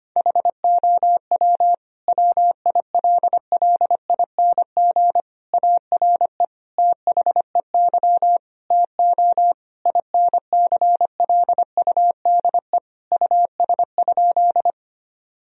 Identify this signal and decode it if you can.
Morse code, heard as a keyed tone, about 25 wpm, 700 Hz. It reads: HOW WILLING ARE THEY TO INCLUDE US?
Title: Sentences from Top 1000 Words - Review Part 3 - 25wpm